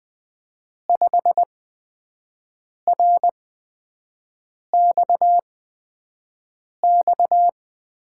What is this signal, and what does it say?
Morse code, heard as a keyed tone, about 20 wpm, 700 Hz. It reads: 5 R X X
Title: USA Call Signs - 5 Characters - 8x Character Spacing - 20wpm